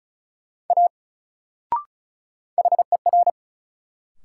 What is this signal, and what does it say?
Morse code, heard as a keyed tone, about 36 wpm, 700 Hz. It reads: A HER
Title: Sentences from Top 100 Words - New Word Drill - 35wpm